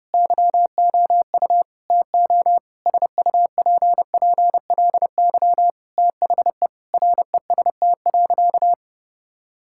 Morse code, heard as a keyed tone, about 32 wpm, 700 Hz. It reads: YOU TO SUPPLY THE REST.
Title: Sentences from Top 1000 Words - Review Part 5 - 30wpm